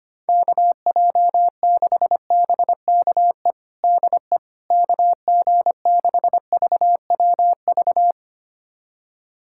Morse code, heard as a keyed tone, about 25 wpm, 700 Hz. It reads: KJ6BKE DE KG6VWV
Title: USA Callsign DE Callsign - 6 Characters - 25wpm